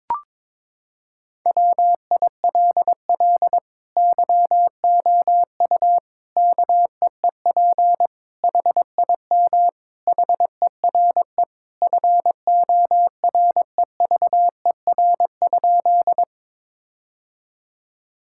Morse code, heard as a keyed tone, about 22 wpm, 700 Hz. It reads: WILL YOU KEEP HIM HERE FOREVER?
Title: Sentences from Top 1600 Words 22wpm